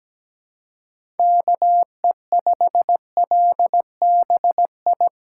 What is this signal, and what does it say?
Morse code, heard as a keyed tone, about 17 wpm, 700 Hz. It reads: KE5LBI
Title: USA Callsign DE Callsign - 6 Characters - 17wpm